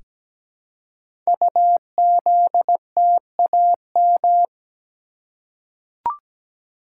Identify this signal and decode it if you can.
Morse code, heard as a keyed tone, about 17 wpm, 700 Hz. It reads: UZTAM
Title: Code Groups - 5 Letters - 17wpm